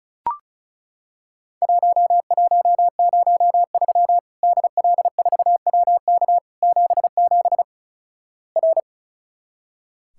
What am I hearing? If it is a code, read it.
Morse code, heard as a keyed tone, about 34 wpm, 700 Hz. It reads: 1103 DL4WK 77 R
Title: Worked All Europe DX Contest - QTC Exchange - 35wpm